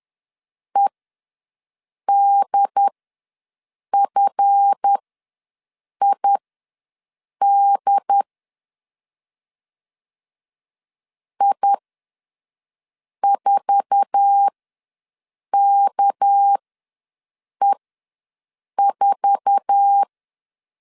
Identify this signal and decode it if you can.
Morse code, heard as a keyed tone, about 10 wpm, spaced as for 6 wpm, 800 Hz. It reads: EDFID I4KE4
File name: LEZIONI / 8 YJ45 / 8 YJ4 ws 10wpm 4020